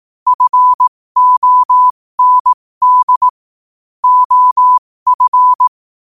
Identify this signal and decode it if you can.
Morse code, heard as a keyed tone, about 18 wpm, spaced as for 15 wpm, 1000 Hz. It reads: FOND OF
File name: qotdfiles / 15wpm